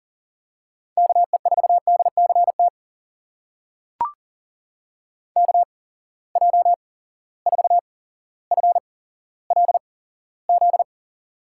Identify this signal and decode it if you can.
Morse code, heard as a keyed tone, about 42 wpm, spaced as for 17 wpm, 700 Hz. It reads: KE4DCT KJ4FLZ
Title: USA Call Signs - 6 Characters - 8x Character Spacing - 40wpm